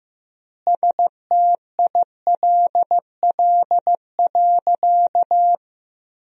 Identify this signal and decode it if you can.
Morse code, heard as a keyed tone, about 15 wpm, 700 Hz. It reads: STILL.
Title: 3x Word-Spacing - Sentences from Top 500 Words - 15wpm